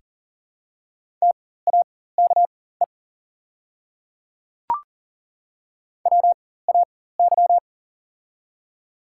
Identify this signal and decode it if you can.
Morse code, heard as a keyed tone, about 40 wpm, spaced as for 19 wpm, 700 Hz. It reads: TAKE WAY
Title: 4x Character Spacing - Top 200 Words - 40wpm